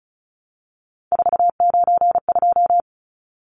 Morse code, heard as a keyed tone, about 34 wpm, 700 Hz. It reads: 492
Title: Numbers - 3-Digit 35wpm